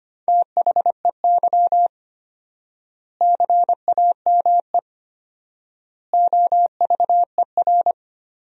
Morse code, heard as a keyed tone, about 25 wpm, 700 Hz. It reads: THEY CAME OVER
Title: 4x Word-Spacing - Sentences from Top 300 Words - 25wpm